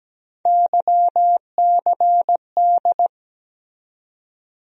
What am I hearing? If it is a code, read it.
Morse code, heard as a keyed tone, about 17 wpm, 700 Hz. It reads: YCD
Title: USA Call Sign Postfixes - 3 Letters 17wpm